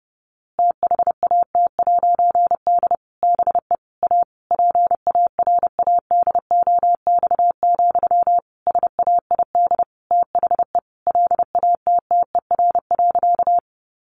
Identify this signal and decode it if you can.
Morse code, heard as a keyed tone, about 28 wpm, 700 Hz. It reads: THAT'D BE A PARADOX, SAID THE LATTER.
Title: A Sound Of Thunder 30wpm